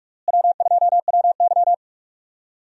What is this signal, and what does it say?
Morse code, heard as a keyed tone, about 46 wpm, 700 Hz. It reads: W2WY
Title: USA Call Signs - 4 Characters - Repeated 3x - 45wpm